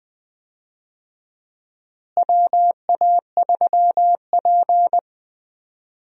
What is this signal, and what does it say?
Morse code, heard as a keyed tone, about 20 wpm, 700 Hz. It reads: WA3P
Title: USA Call Signs - 4 Characters - Repeated 3x - 20wpm